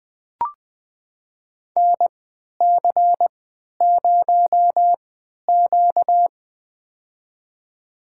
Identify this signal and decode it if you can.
Morse code, heard as a keyed tone, about 20 wpm, spaced as for 11 wpm, 700 Hz. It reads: NC0Q
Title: USA Call Signs - 3x Character Spacing - 20wpm